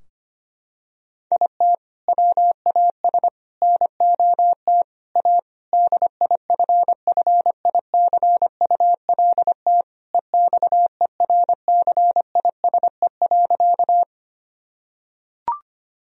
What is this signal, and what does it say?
Morse code, heard as a keyed tone, about 25 wpm, 700 Hz. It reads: IT WAS NOT A DIFFICULT EXERCISE.
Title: Sentences from Top 1000 Words - Review Part 4 - 25wpm